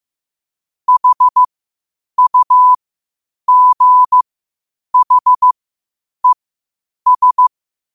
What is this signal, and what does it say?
Morse code, heard as a keyed tone, about 15 wpm, spaced as for 8 wpm, 1000 Hz. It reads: HUGHES
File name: qotdfiles / 8wpm